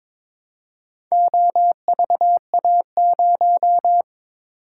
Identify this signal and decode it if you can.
Morse code, heard as a keyed tone, about 21 wpm, 700 Hz. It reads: OVA0
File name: practice / 22wpm / Four Letter-Number Combinations - 22wpm